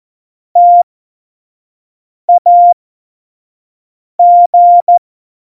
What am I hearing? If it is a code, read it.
Morse code, heard as a keyed tone, about 14 wpm, spaced as for 5 wpm, 700 Hz. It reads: TAG
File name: Morsekurs / Witze / Witz 28. Januar 2020, 25 F70